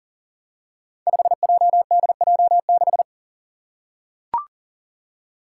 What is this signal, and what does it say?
Morse code, heard as a keyed tone, about 40 wpm, 700 Hz. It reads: 5JDJ6
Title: Five Challenging Letter-Number Combinations 40wpm